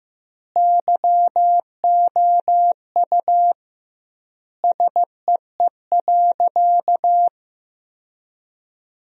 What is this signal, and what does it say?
Morse code, heard as a keyed tone, about 15 wpm, 700 Hz. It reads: YOU SEE.
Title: 2x Word-Spacing - Sentences from Top 100 Words - 15wpm